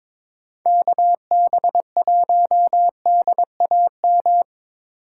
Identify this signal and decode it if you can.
Morse code, heard as a keyed tone, about 22 wpm, 700 Hz. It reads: KB1DAM
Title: USA Call Signs - 6 Characters - 4x Character Spacing - 22wpm